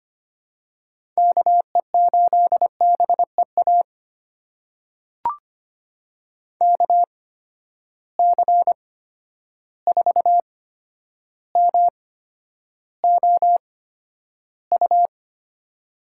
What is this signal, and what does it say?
Morse code, heard as a keyed tone, about 25 wpm, spaced as for 11 wpm, 700 Hz. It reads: KE8BEA KC4MOU
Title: USA Call Signs - 6 Characters - 8x Character Spacing - 25wpm